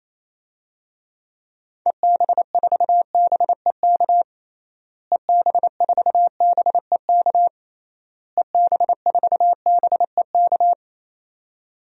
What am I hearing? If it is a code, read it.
Morse code, heard as a keyed tone, about 27 wpm, 700 Hz. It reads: EB4BEK EB4BEK EB4BEK
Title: DXCC-100 Call Signs - 6 Characters - Repeated 3x - 28wpm